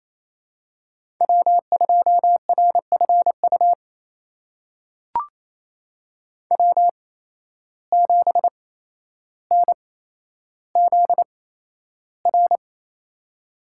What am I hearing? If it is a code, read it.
Morse code, heard as a keyed tone, about 28 wpm, spaced as for 12 wpm, 700 Hz. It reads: W2RFU W7NZR
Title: USA Call Signs - 5 Characters - 8x Character Spacing - 28wpm